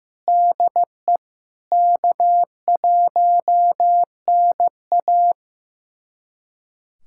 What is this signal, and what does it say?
Morse code, heard as a keyed tone, about 16 wpm, 700 Hz. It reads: DE K1NA
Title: USA Callsign DE Callsign - 4 Characters - 15wpm